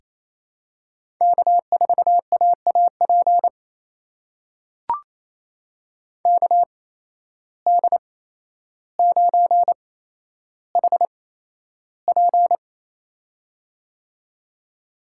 Morse code, heard as a keyed tone, about 28 wpm, spaced as for 12 wpm, 700 Hz. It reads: K4AAP KD9HP
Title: USA Call Signs - 5 Characters - 8x Character Spacing - 28wpm